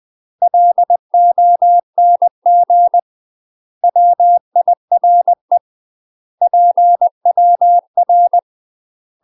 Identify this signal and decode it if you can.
Morse code, heard as a keyed tone, about 20 wpm, 700 Hz. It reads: LONG WIRE PWR
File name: UK MASTER QSOs TRAINING / 20WPM / UK MASTER QSOs 0043